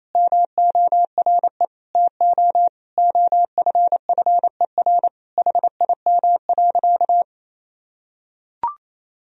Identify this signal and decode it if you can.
Morse code, heard as a keyed tone, about 28 wpm, 700 Hz. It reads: MORE TO OFFER HIM.
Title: Sentences from Top 1000 Words - Review Part 4 - 28wpm